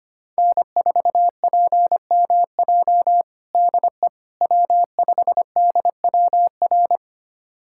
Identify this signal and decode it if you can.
Morse code, heard as a keyed tone, about 25 wpm, 700 Hz. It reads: N4PMJ DE W5DWR
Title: USA Callsign DE Callsign - 5 Characters - 25wpm